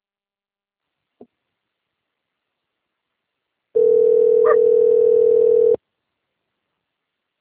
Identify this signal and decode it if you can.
Ringback tone (North America), 440 + 480 Hz.
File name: out- - - - - .41904